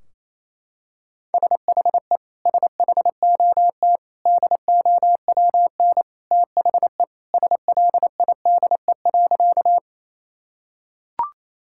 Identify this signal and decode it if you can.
Morse code, heard as a keyed tone, about 28 wpm, 700 Hz. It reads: SHE SHOT DOWN THE SLIDE.
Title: Sentences from Top 1600 Words 28wpm